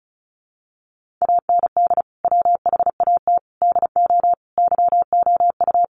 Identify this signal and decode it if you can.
Morse code, heard as a keyed tone, about 34 wpm, 700 Hz. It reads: AND WHAT DO YOU
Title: Sentences from Top 100 Words 35wpm